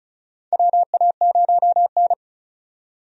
Morse code, heard as a keyed tone, about 34 wpm, 700 Hz. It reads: WA0N
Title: USA Call Signs - 4 Characters - 35wpm